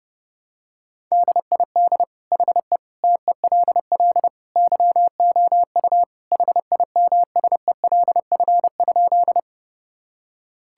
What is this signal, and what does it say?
Morse code, heard as a keyed tone, about 30 wpm, 700 Hz. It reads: DID HE TELL YOU HIMSELF?
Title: Sentences from Top 1600 Words 30wpm